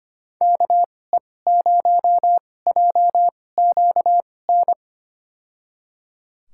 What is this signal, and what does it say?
Morse code, heard as a keyed tone, about 25 wpm, spaced as for 18 wpm, 700 Hz. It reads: KE0JQN
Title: USA Call Signs - 6 Characters - 2x Character Spacing - 25wpm